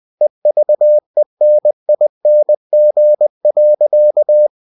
Morse code, heard as a keyed tone, about 20 wpm, 600 Hz. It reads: EVENING.
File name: cw-books / winnie-the-pooh / winnie-20wpm-page-0071